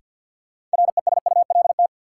Morse code, heard as a keyed tone, about 52 wpm, 700 Hz. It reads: RESULT
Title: Sets of 3 Words - New Word Drill - 50wpm